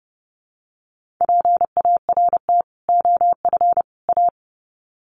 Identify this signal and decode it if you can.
Morse code, heard as a keyed tone, about 30 wpm, 700 Hz. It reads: PART OF A
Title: Sets of 3 Words 30wpm